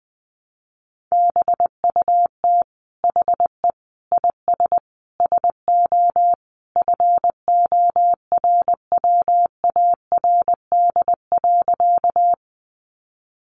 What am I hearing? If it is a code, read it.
Morse code, heard as a keyed tone, about 20 wpm, 700 Hz. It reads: BUT HE IS SO FORWARD.
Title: Sentences from Top 1300 Words 20wpm